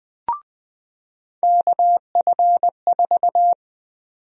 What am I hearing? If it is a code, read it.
Morse code, heard as a keyed tone, about 20 wpm, 700 Hz. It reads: KF4
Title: USA Call Sign Prefixes - 20wpm